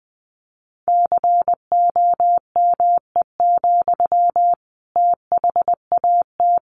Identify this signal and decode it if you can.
Morse code, heard as a keyed tone, about 20 wpm, 700 Hz. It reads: COME, THAT
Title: Sentences from Top 700 Words 20wpm